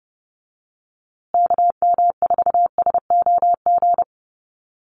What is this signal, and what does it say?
Morse code, heard as a keyed tone, about 30 wpm, 700 Hz. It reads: KM4SOG
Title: USA Call Signs Repeated 3x 30wpm